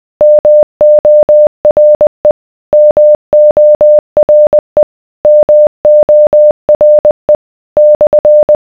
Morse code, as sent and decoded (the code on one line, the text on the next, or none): -- --- .-. / -- --- .-. / -- --- .-. / -..-.
MORE MORE MORE /